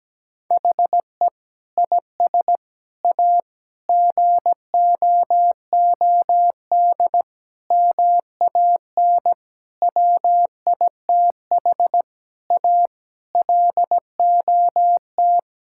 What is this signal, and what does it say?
Morse code, heard as a keyed tone, about 17 wpm, 700 Hz. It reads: HE IS A GOOD MAN WITH A LOT
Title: Sentences from Top 1000 Words - Review Part 6 - 17wpm